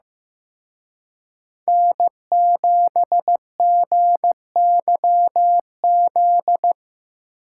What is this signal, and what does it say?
Morse code, heard as a keyed tone, about 17 wpm, 700 Hz. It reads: N7GYZ
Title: Speed Racing - USA 1x3 Call Signs 15wpm